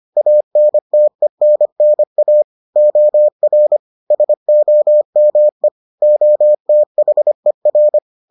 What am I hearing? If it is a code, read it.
Morse code, heard as a keyed tone, about 25 wpm, 600 Hz. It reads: ANTENNA OR SOME OTHER